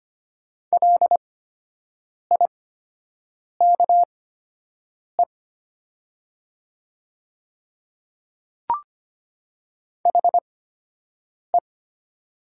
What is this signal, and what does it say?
Morse code, heard as a keyed tone, about 25 wpm, spaced as for 7 wpm, 700 Hz. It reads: LIKE HE
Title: 8x Character Spacing - Top 100 Words - 25wpm